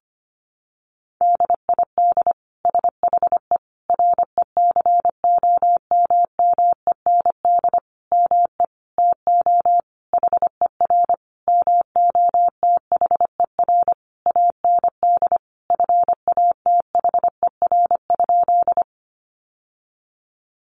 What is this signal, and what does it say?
Morse code, heard as a keyed tone, about 25 wpm, 700 Hz. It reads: DID SHE RECOMMEND ME TO HER MOTHER AND FATHER?
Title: Sentences from Top 1200 Words 25wpm